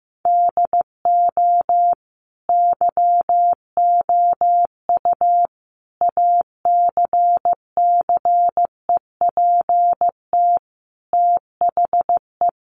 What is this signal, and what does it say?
Morse code, heard as a keyed tone, about 15 wpm, 700 Hz. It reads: DO YOU ACCEPT THE
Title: Sentences from Top 1300 Words 15wpm